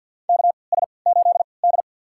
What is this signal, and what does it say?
Morse code, heard as a keyed tone, about 50 wpm, spaced as for 28 wpm, 700 Hz. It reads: KS8D
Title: USA Call Signs - 4 Characters - 3x Character Spacing - 50wpm